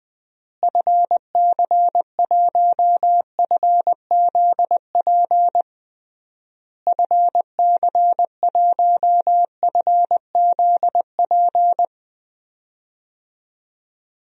Morse code, heard as a keyed tone, about 20 wpm, 700 Hz. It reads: FC1FZP FC1FZP
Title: DXCC-100 Call Signs - 6 Characters - Repeated 3x - 20wpm